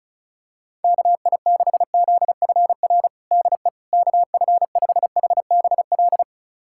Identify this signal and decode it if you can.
Morse code, heard as a keyed tone, about 36 wpm, 700 Hz. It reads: KI6ZFR DE KF5HBL
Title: USA Callsign DE Callsign - 6 Characters - 35wpm